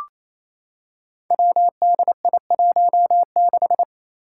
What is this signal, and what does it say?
Morse code, heard as a keyed tone, about 27 wpm, 700 Hz. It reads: WDI16
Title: Five Challenging Letter-Number Combinations 28wpm